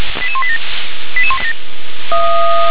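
Five-tone selective call (ZVEI), heard as quick runs of five tones, tones of about 70 ms. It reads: 88187
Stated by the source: ZVEI five-tone call 88161 received on SDR.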